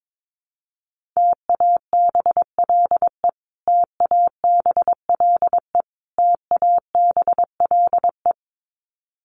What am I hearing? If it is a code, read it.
Morse code, heard as a keyed tone, about 22 wpm, 700 Hz. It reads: TABLE TABLE TABLE
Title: Top 500 Words Repeated 22wpm